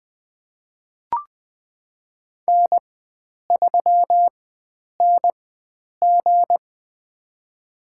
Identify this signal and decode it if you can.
Morse code, heard as a keyed tone, about 20 wpm, spaced as for 9 wpm, 700 Hz. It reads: N3NG
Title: USA Call Signs - 4 Characters - 4x Character Spacing - 20wpm